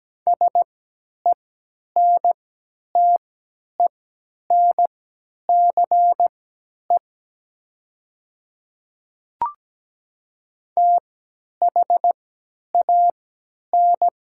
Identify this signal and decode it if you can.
Morse code, heard as a keyed tone, about 17 wpm, spaced as for 10 wpm, 700 Hz. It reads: SENTENCE THAN
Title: 3x Character Spacing - Top 300 Words - 17wpm